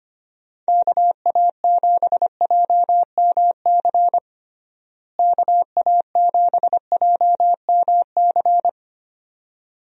Morse code, heard as a keyed tone, about 25 wpm, 700 Hz. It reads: KA7JMC KA7JMC
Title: USA Call Signs - 6 Characters - Repeated 2x - 25wpm